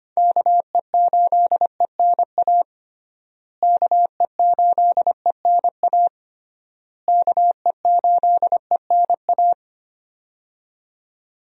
Morse code, heard as a keyed tone, about 25 wpm, 700 Hz. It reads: KE8ENA KE8ENA KE8ENA
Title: USA Call Signs - 6 Characters - Repeated 3x - 25wpm